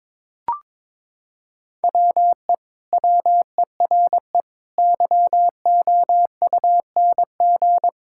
Morse code, heard as a keyed tone, about 22 wpm, 700 Hz. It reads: WE WERE YOUNG
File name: practice / 22wpm / Sentences with 2-Word Phrases - 22wpm